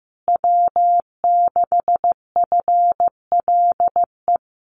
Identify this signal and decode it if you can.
Morse code, heard as a keyed tone, about 15 wpm, 700 Hz. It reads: W6FLE
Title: Common QSO Phrases Encore 15wpm